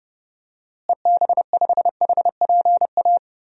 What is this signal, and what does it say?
Morse code, heard as a keyed tone, about 32 wpm, 700 Hz. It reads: EB5HPA
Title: DXCC-100 Call Signs - 6 Characters - Repeated 3x - 30wpm